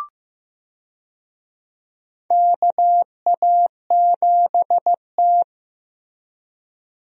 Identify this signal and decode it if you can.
Morse code, heard as a keyed tone, about 16 wpm, 700 Hz. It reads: KA7T
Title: USA Call Signs - 4 Characters - Repeated 2x - 15wpm